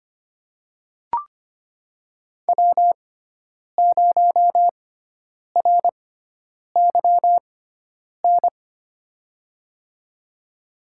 Morse code, heard as a keyed tone, about 25 wpm, spaced as for 9 wpm, 700 Hz. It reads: W0RYN
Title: USA Call Signs - 5 Characters - 6x Character Spacing - 25wpm